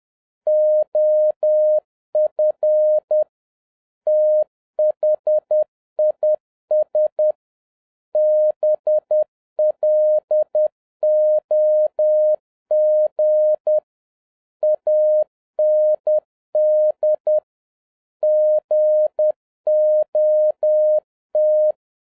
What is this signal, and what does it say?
Morse code, heard as a keyed tone, about 10 wpm, 610 Hz. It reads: OF THIS BLOG AND GOT
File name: 2011-07-mp3 / n46 JOHN K3TN 10wpm